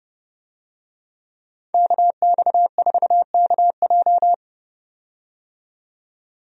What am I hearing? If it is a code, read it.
Morse code, heard as a keyed tone, about 30 wpm, 700 Hz. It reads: KX4KJ
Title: USA Call Signs - 5 Characters - 30wpm